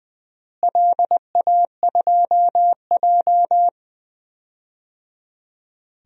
Morse code, heard as a keyed tone, about 20 wpm, 700 Hz. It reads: LA2J
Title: DXCC-100 Call Signs - 4 Characters - Repeated 3x - 20wpm